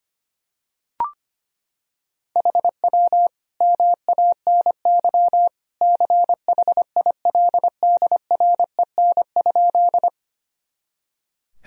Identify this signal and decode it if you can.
Morse code, heard as a keyed tone, about 25 wpm, 700 Hz. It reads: HW MANY CHILDREN?
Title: Sentences from Top 300 Words and Abbreviations 25wpm